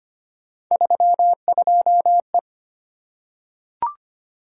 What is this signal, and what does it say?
Morse code, heard as a keyed tone, about 25 wpm, 700 Hz. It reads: 32E
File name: Three Letter-Number Combinations - 25wpm